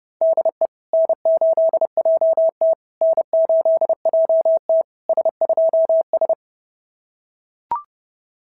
Morse code, heard as a keyed tone, about 30 wpm, 650 Hz. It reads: DE N8JT N8JT S2S
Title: SOTA QSO Practice 30wpm